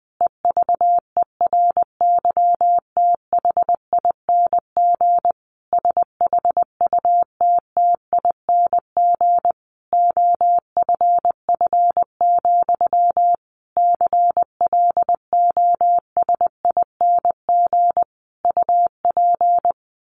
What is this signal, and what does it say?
Morse code, heard as a keyed tone, about 20 wpm, 700 Hz. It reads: EVERYTHING SHUTTING OFF, CLOSING UP
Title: A Sound Of Thunder 20wpm